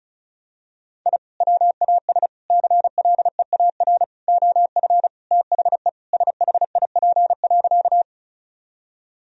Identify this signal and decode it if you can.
Morse code, heard as a keyed tone, about 36 wpm, 700 Hz. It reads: I WAS CLEAR OF THE SHIP.
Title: Sentences from Top 1000 Words - Review Part 2 - 35wpm